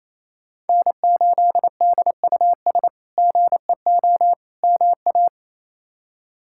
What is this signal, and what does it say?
Morse code, heard as a keyed tone, about 28 wpm, 700 Hz. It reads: N8DUS GEO MA
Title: K1USN SST QSO Practice - 28wpm